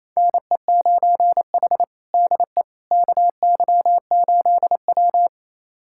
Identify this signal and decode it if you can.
Morse code, heard as a keyed tone, about 28 wpm, 700 Hz. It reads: NE9H DE KY8W
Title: USA Callsign DE Callsign - 4 Characters - 28wpm